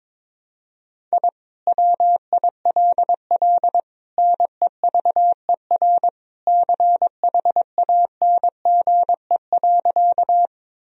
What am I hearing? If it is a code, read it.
Morse code, heard as a keyed tone, about 22 wpm, 700 Hz. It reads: I WILL NEVER CHANGE.
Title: Sentences from Top 1000 Words - Review Part 1 - 22wpm